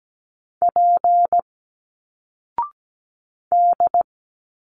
Morse code, heard as a keyed tone, about 17 wpm, 700 Hz. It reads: P D
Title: Single Letters 17wpm